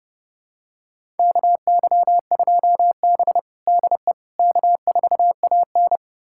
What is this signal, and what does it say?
Morse code, heard as a keyed tone, about 29 wpm, 700 Hz. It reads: KY2B DE K4AN
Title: USA Callsign DE Callsign - 4 Characters - 30wpm